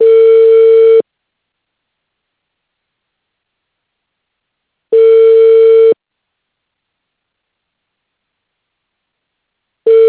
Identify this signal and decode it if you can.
Ringback tone (China, 450 Hz).